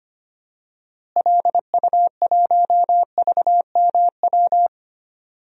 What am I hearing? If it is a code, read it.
Morse code, heard as a keyed tone, about 25 wpm, 700 Hz. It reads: LU1VMW
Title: DXCC-100 Call Signs - 6 Characters - Repeated 3x - 25wpm